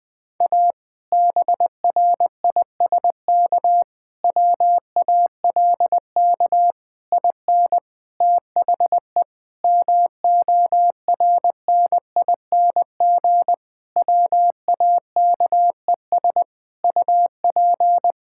Morse code, heard as a keyed tone, about 19 wpm, 700 Hz. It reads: A BRISK WALK IN THE MORNING WAKES UP
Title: Difficult Sentences with Hints - 20wpm